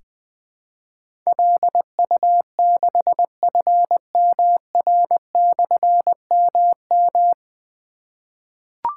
Morse code, heard as a keyed tone, about 20 wpm, 700 Hz. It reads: LU6FMR/MM
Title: Difficult Call Signs 20wpm